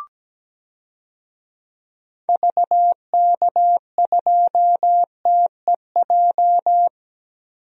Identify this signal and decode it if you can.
Morse code, heard as a keyed tone, about 17 wpm, 700 Hz. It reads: VK2TEJ
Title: DXCC-100 Call Signs - 6 Characters - Repeated 2x - 17wpm